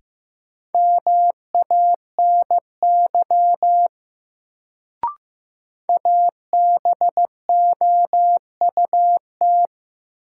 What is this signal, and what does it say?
Morse code, heard as a keyed tone, about 15 wpm, 700 Hz. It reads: MANY ABOUT
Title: Sentences from Top 100 Words - New Word Drill - 15wpm